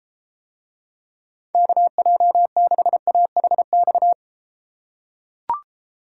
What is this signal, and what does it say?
Morse code, heard as a keyed tone, about 33 wpm, 700 Hz. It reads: KJ6AHX